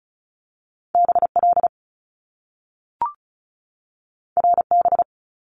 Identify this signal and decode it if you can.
Morse code, heard as a keyed tone, about 36 wpm, 700 Hz. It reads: BL RB
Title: Top Two Letter Combinations in Words 35wpm